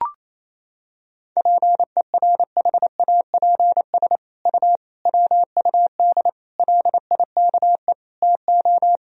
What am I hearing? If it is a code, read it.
Morse code, heard as a keyed tone, about 28 wpm, 700 Hz. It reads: PERHAPS U WUD LIKE TO
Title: Sentences from Top 500 Words and Abbreviations 28wpm